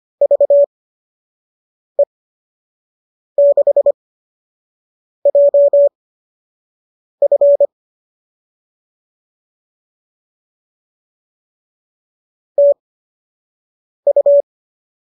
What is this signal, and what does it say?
Morse code, heard as a keyed tone, about 25 wpm, spaced as for 6 wpm, 590 Hz. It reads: VE6JF TU
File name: qso002 06